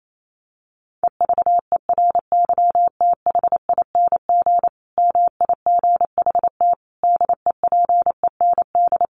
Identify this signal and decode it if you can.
Morse code, heard as a keyed tone, about 28 wpm, 700 Hz. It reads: EVERYTHING MIGHT DEPEND
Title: Sentences from Top 1300 Words 28wpm